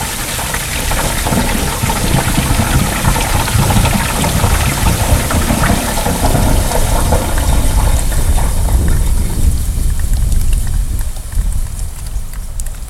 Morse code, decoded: none

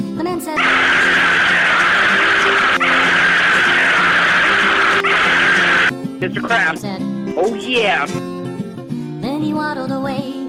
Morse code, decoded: none

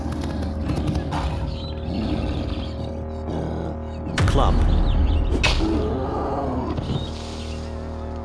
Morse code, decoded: none

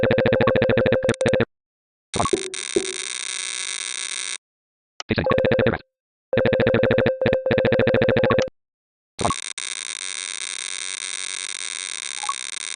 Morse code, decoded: none

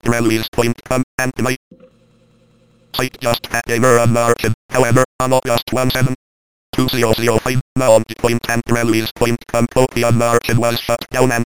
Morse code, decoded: none

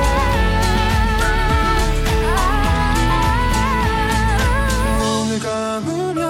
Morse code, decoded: none

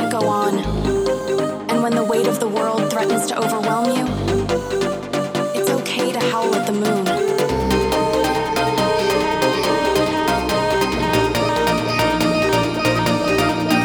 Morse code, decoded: none

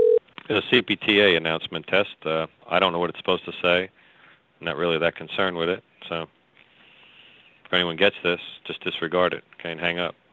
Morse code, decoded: none

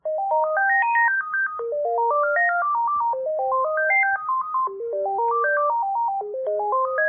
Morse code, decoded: none

